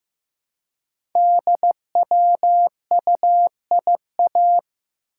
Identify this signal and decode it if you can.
Morse code, heard as a keyed tone, about 15 wpm, 700 Hz. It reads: DWUIA
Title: Code Groups - 5 Letter:Numbers - 15wpm